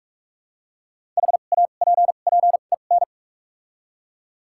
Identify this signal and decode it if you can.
Morse code, heard as a keyed tone, about 45 wpm, spaced as for 32 wpm, 700 Hz. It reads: HAPPEN
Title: 2x Character Spacing - Top 500 Words - 45wpm